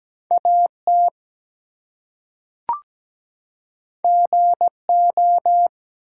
Morse code, heard as a keyed tone, about 17 wpm, 700 Hz. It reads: AT GO
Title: Two Letter Words - 17wpm